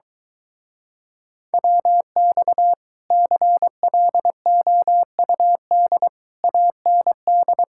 Morse code, heard as a keyed tone, about 24 wpm, 700 Hz. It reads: WX CLOUD AND